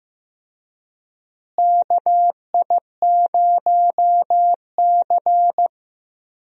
Morse code, heard as a keyed tone, about 15 wpm, 700 Hz. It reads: KI0C
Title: USA Call Signs - 4 Characters - Repeated 2x - 15wpm